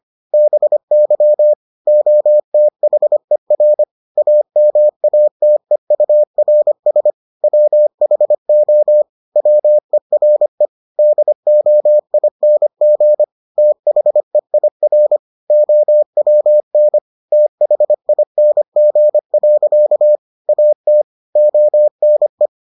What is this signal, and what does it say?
Morse code, heard as a keyed tone, about 25 wpm, 600 Hz. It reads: BY OTHER AMATEURS WHO WERE DOING THEIR OWN THING. AT ONE